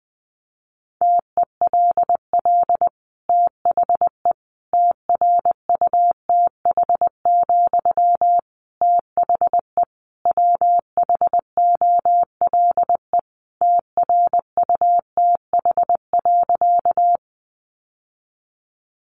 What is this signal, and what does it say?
Morse code, heard as a keyed tone, about 20 wpm, 700 Hz. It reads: TELL THE TRUTH, THE WHOLE TRUTH.
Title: Sentences from Top 800 Words 20wpm